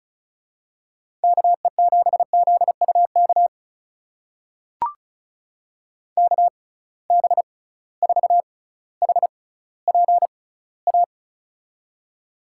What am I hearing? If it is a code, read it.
Morse code, heard as a keyed tone, about 36 wpm, spaced as for 18 wpm, 700 Hz. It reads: KE7ZUK KB4HPA
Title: USA Call Signs - 6 Characters - 6x Character Spacing - 35wpm